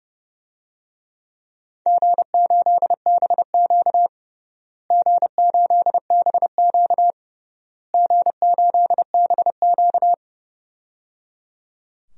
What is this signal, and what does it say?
Morse code, heard as a keyed tone, about 30 wpm, 700 Hz. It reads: G8BQ G8BQ G8BQ
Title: DXCC-100 Call Signs - 4 Characters - Repeated 3x - 30wpm